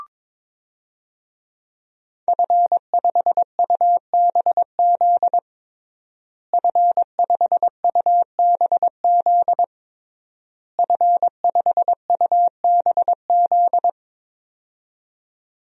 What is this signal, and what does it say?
Morse code, heard as a keyed tone, about 22 wpm, 700 Hz. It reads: F5UBZ F5UBZ F5UBZ
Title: DXCC-100 Call Signs - 5 Characters - Repeated 3x - 22wpm